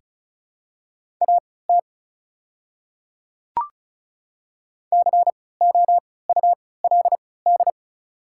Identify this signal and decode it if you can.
Morse code, heard as a keyed tone, about 36 wpm, spaced as for 20 wpm, 690 Hz. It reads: AT COULD
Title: 3x Character Spacing - Top 100 Words - 35wpm